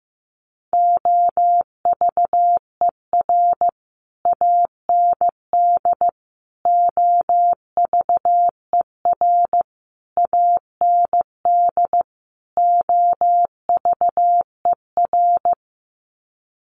Morse code, heard as a keyed tone, about 15 wpm, 700 Hz. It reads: OVER AND OVER AND OVER
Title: Sets of 5 Words Encore 15wpm